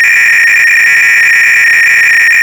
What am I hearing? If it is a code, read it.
300-baud modem data.